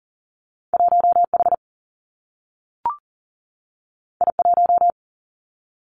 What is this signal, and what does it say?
Morse code, heard as a keyed tone, about 38 wpm, 700 Hz. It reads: 1H I1
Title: Two Challenging Letter-Number Combinations 40wpm